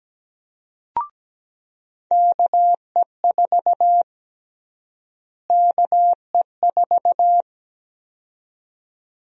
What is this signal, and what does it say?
Morse code, heard as a keyed tone, about 17 wpm, 700 Hz. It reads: KE4 KE4
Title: USA Call Sign Prefixes - Repeated 2x - 17wpm